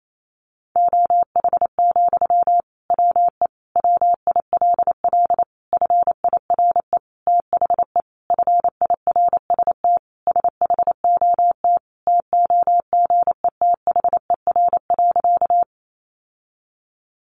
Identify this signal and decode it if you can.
Morse code, heard as a keyed tone, about 28 wpm, 700 Hz. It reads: OH, WE WILL FIRE THE FIRST SHOT TOGETHER.
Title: Sentences from Top 1100 Words 28wpm